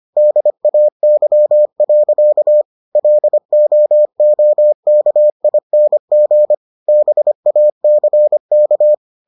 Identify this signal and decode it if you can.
Morse code, heard as a keyed tone, about 25 wpm, 600 Hz. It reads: DAY. LOOKING BACK